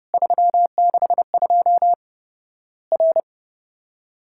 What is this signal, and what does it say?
Morse code, heard as a keyed tone, about 30 wpm, 700 Hz. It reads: 362 R
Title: Worked All Europe DX Contest - QTC Exchange - 30wpm